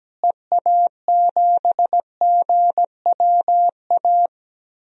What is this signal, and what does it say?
Morse code, heard as a keyed tone, about 17 wpm, 700 Hz. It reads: EA7GWA
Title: DXCC-100 Call Signs - 6 Characters - Repeated 3x - 17wpm